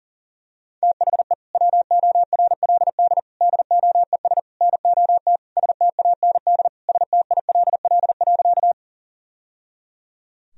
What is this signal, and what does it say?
Morse code, heard as a keyed tone, about 40 wpm, 700 Hz. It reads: THE WORLD DOES NOT STAND STILL.
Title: Sentences from Top 1000 Words - Review Part 1 - 40wpm